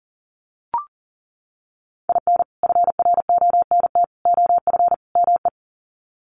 Morse code, heard as a keyed tone, about 38 wpm, 700 Hz. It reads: IN FRONT OF ME
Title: Sets of 4 Words 40wpm